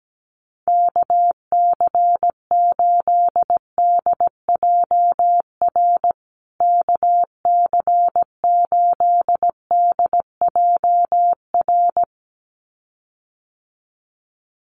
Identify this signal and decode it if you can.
Morse code, heard as a keyed tone, about 17 wpm, 700 Hz. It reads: KC8DJR KC8DJR
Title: USA Call Signs Repeated 2x 17wpm